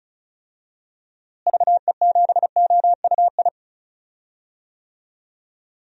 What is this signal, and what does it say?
Morse code, heard as a keyed tone, about 36 wpm, 700 Hz. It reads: VE7OUI
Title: DXCC-100 Call Signs - 6 Characters - Repeated 3x - 35wpm